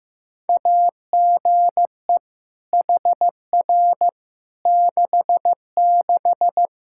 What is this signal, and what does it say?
Morse code, heard as a keyed tone, about 15 wpm, 700 Hz. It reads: AGE HR 66
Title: Practice QSO - Exchange 3 - More Info - 15wpm